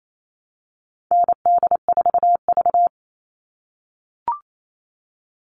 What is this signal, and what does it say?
Morse code, heard as a keyed tone, about 28 wpm, 700 Hz. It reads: ND4V
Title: All Cut Numbers - 4-Digit 28wpm